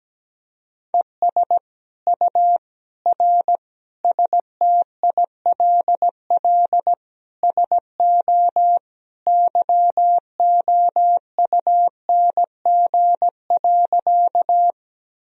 Morse code, heard as a keyed tone, about 17 wpm, 700 Hz. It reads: ES U R STILL SO YOUNG.
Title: Sentences from Top 400 Words and Abbreviations 17wpm